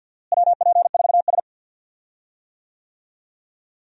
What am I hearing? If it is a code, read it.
Morse code, heard as a keyed tone, about 52 wpm, 700 Hz. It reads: WP4S
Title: USA Call Signs - 4 Characters - Repeated 3x - 50wpm